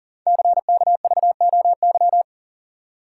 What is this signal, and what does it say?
Morse code, heard as a keyed tone, about 38 wpm, 690 Hz. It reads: CKVOY
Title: Code Groups - 5 Letters - 40wpm